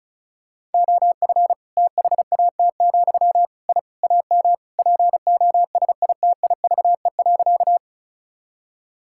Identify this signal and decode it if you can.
Morse code, heard as a keyed tone, about 36 wpm, 700 Hz. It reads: OF THAT, I AM POSITIVE.
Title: Sentences from Top 1000 Words - Review Part 5 - 35wpm